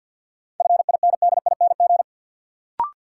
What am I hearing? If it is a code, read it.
Morse code, heard as a keyed tone, about 52 wpm, 700 Hz. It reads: FINDING
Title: Sentences from Top 1100 Words - New Word Drill Alt - 50wpm